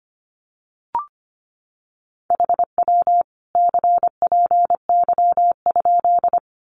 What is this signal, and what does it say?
Morse code, heard as a keyed tone, about 25 wpm, 700 Hz. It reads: HW CPY?
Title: Common QSO Phrases 25wpm